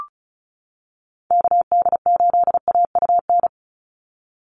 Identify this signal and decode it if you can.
Morse code, heard as a keyed tone, about 34 wpm, 700 Hz. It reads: KD8AUN
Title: USA Call Signs 35wpm